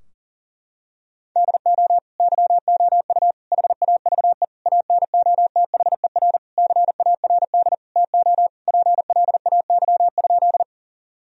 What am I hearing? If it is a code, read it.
Morse code, heard as a keyed tone, about 40 wpm, 700 Hz. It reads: DO YOU HAVE ANOTHER CARD TO PLAY?
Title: Sentences from Top 1000 Words - Review Part 4 - 40wpm